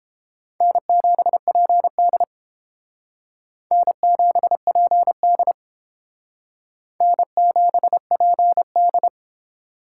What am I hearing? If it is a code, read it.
Morse code, heard as a keyed tone, about 28 wpm, 700 Hz. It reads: N7PD N7PD N7PD